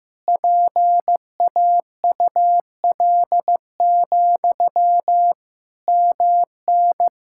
Morse code, heard as a keyed tone, about 15 wpm, 700 Hz. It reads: PAUL, MN
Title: Practice QSO - Exchange 1 - Beginning - 15wpm